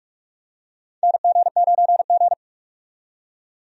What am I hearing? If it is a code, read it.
Morse code, heard as a keyed tone, about 46 wpm, 700 Hz. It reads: NG9G
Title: USA Call Signs - 4 Characters - 45wpm